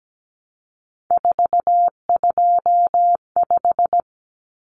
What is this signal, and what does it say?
Morse code, heard as a keyed tone, about 17 wpm, 700 Hz. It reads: 425
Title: Numbers - 3-Digit 17wpm